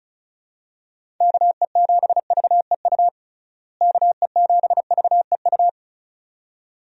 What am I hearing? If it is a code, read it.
Morse code, heard as a keyed tone, about 36 wpm, 700 Hz. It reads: KE7VEU KE7VEU
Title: USA Call Signs - 6 Characters - Repeated 2x - 35wpm